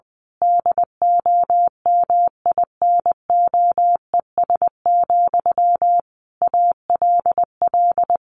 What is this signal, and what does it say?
Morse code, heard as a keyed tone, about 20 wpm, 700 Hz. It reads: DOMINOES, ALL
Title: A Sound Of Thunder 20wpm